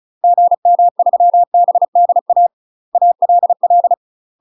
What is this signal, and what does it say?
Morse code, heard as a keyed tone, about 36 wpm, 700 Hz. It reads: GM3BDA ALL
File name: UK MASTER QSOs TRAINING / 35WPM / UK MASTER QSOs 0035